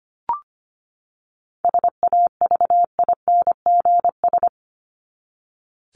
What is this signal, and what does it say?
Morse code, heard as a keyed tone, about 25 wpm, 700 Hz. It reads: SAVINGS
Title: Top 1001-1250 Words 25wpm